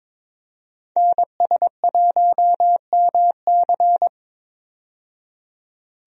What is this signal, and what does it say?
Morse code, heard as a keyed tone, about 22 wpm, 700 Hz. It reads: NS1MC
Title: USA Call Signs - 5 Characters - 22wpm